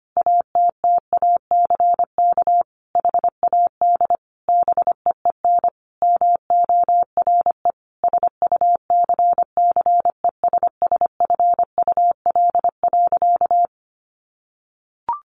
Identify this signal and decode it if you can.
Morse code, heard as a keyed tone, about 25 wpm, 700 Hz. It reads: ATTACK HAD BEEN MORE SUCCESSFUL.
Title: Sentences from Top 1500 Words 25wpm